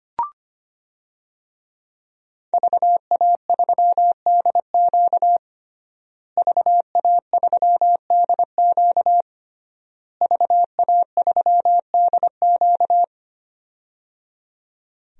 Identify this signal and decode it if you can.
Morse code, heard as a keyed tone, about 25 wpm, 700 Hz. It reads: VA3DQ VA3DQ VA3DQ
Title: DXCC-100 Call Signs - 5 Characters - Repeated 3x - 25wpm